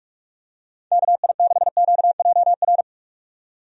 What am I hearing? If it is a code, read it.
Morse code, heard as a keyed tone, about 44 wpm, 700 Hz. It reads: KI6QJR
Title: USA Call Signs - 6 Characters - 45wpm